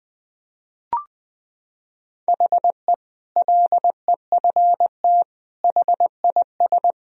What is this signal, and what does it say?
Morse code, heard as a keyed tone, about 20 wpm, 700 Hz. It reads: HE LEFT HIS
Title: Sentences from Top 1000 Words - Review Part 2 - 20wpm